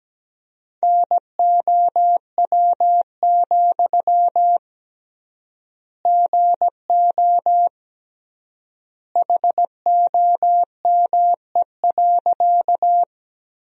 Morse code, heard as a keyed tone, about 17 wpm, 700 Hz. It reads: NOW, GO HOME.
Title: 3x Word-Spacing - Sentences from Top 200 Words - 17wpm